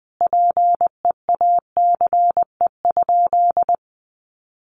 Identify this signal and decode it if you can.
Morse code, heard as a keyed tone, about 20 wpm, 700 Hz. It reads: PEACE?
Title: Sentences from Top 1500 Words 20wpm